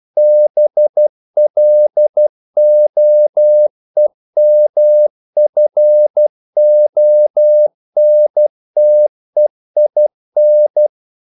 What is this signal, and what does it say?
Morse code, heard as a keyed tone, about 12 wpm, 600 Hz. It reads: BLOEMFONTEIN
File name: CW / Practice / NationalCapitals12